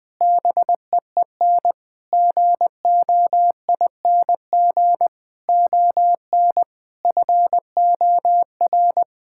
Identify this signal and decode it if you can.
Morse code, heard as a keyed tone, about 20 wpm, 700 Hz. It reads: BEEN GOING ON FOR
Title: Sentences with 8-Word Phrases - Spoken Beginning - 20wpm